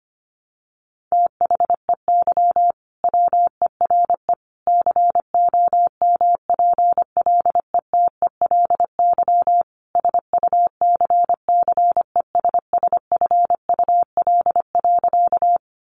Morse code, heard as a keyed tone, about 25 wpm, 700 Hz. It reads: THEY WERE COMPLETELY SUCCESSFUL.
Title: Sentences from Top 1100 Words 25wpm